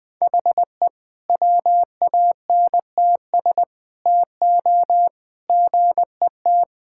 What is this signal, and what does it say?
Morse code, heard as a keyed tone, about 19 wpm, 700 Hz. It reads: HE WANTS TO GET
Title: Sentences with 4-Word Phrases - Spoken Beginning - 20wpm